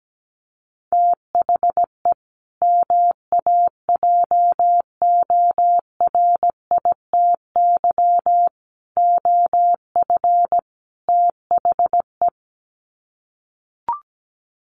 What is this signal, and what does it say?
Morse code, heard as a keyed tone, about 17 wpm, 700 Hz. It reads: THE MAJORITY OF THE
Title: Sets of 4 Words 17wpm